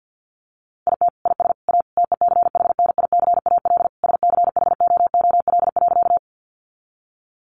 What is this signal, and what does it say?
Morse code, heard as a keyed tone, about 50 wpm, 700 Hz. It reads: IT IS A TECHNICAL SCHOOL.
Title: Sentences from Top 1100 Words 50wpm